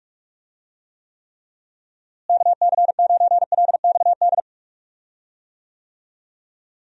Morse code, heard as a keyed tone, about 42 wpm, 700 Hz. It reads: KC9LXD
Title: USA Call Signs - 6 Characters - Repeated 3x - 45wpm